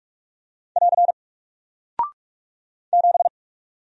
Morse code, heard as a keyed tone, about 44 wpm, 700 Hz. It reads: <AR> 7
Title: Single Character (Letter:Number:Punctuation:Prosign) Alt 45wpm